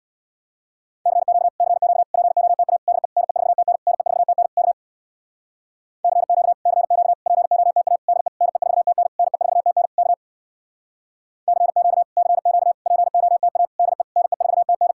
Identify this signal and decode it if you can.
Morse code, heard as a keyed tone, about 61 wpm, 700 Hz. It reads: CQ CQ POTA DE NE4TN NE4TN K CQ CQ POTA DE NE4TN NE4TN K CQ CQ POTA DE NE4TN
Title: Speed-Racing Common QSO Phrases - 50wpm